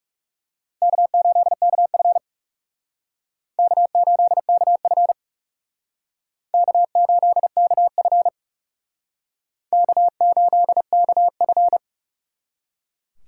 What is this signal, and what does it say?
Morse code, heard as a keyed tone, about 40 wpm, 700 Hz. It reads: K8KF K8KF K8KF K8KF